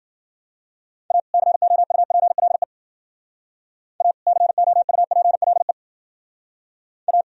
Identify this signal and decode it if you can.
Morse code, heard as a keyed tone, about 49 wpm, 700 Hz. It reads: A COUPLE A COUPLE A
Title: Speed-Racing - Sets of 2 Words 40wpm